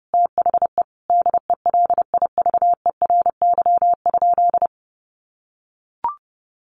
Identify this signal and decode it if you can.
Morse code, heard as a keyed tone, about 28 wpm, 700 Hz. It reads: THE DELIVERY?
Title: Sentences from Top 1500 Words 30wpm